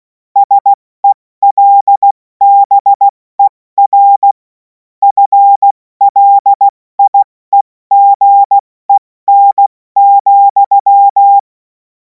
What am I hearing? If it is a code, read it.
Morse code, heard as a keyed tone, about 16 wpm, 800 Hz. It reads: SELBER FLIEGEN,
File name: Morsekurs / Witze / Witz 31. März 2026 70 F80